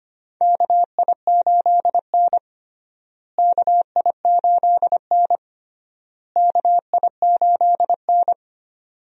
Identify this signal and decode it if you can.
Morse code, heard as a keyed tone, about 25 wpm, 700 Hz. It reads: KI8N KI8N KI8N
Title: USA Call Signs - 4 Characters - Repeated 3x - 25wpm